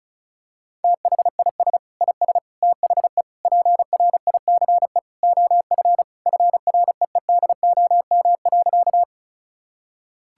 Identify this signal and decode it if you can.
Morse code, heard as a keyed tone, about 34 wpm, 700 Hz. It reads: THIS IS THE PRICE OF FREEDOM.
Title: Sentences from Top 1000 Words - Review Part 5 - 35wpm